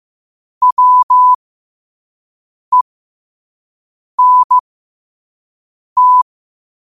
Morse code, heard as a keyed tone, about 15 wpm, spaced as for 5 wpm, 1000 Hz. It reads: WENT
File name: qotdfiles / 5wpm